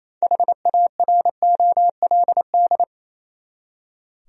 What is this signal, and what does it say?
Morse code, heard as a keyed tone, about 28 wpm, 700 Hz. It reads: HAROLD
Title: USA Common Names - 28wpm